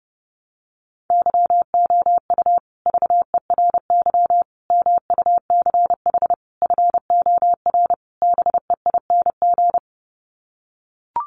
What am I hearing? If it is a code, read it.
Morse code, heard as a keyed tone, about 30 wpm, 700 Hz. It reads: YOU VERY MUCH FOR BEING
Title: Sets of 5 Words Encore 30wpm